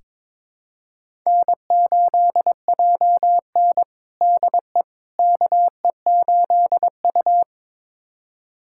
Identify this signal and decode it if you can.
Morse code, heard as a keyed tone, about 22 wpm, 700 Hz. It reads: N8JN DE KE8U
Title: USA Callsign DE Callsign - 4 Characters - 22wpm